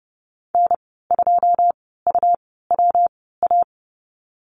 Morse code, heard as a keyed tone, about 30 wpm, spaced as for 17 wpm, 700 Hz. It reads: N2UWA
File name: Lesson 19g - Letter w - Call Signs 30wpm